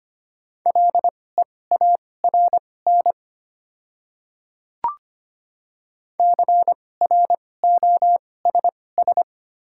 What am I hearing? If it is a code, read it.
Morse code, heard as a keyed tone, about 25 wpm, spaced as for 18 wpm, 700 Hz. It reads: LEARN CROSS
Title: 2x Character Spacing - Top 300 Words - 25wpm